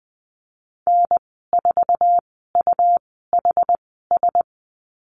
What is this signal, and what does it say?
Morse code, heard as a keyed tone, about 20 wpm, spaced as for 14 wpm, 700 Hz. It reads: N4UHS
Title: 128 Lesson 21g - Number 6 - Call Signs 20wpm